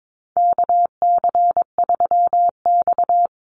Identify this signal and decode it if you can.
Morse code, heard as a keyed tone, about 22 wpm, 690 Hz. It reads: KC3X
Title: Call Signs Repeated 3x - 22wpm